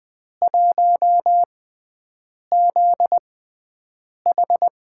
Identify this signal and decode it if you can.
Morse code, heard as a keyed tone, about 21 wpm, spaced as for 7 wpm, 700 Hz. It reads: 1ZH
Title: USA Call Signs - 6 Characters - 6x Character Spacing - 20wpm